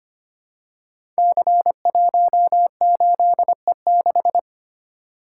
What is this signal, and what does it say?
Morse code, heard as a keyed tone, about 25 wpm, 700 Hz. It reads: C18E6
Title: Code Groups - 5 Letter:Numbers - 25wpm